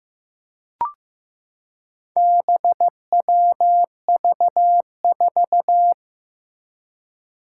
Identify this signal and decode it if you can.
Morse code, heard as a keyed tone, about 15 wpm, 700 Hz. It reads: BWV4
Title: Four Challenging Letter-Number Combinations 15wpm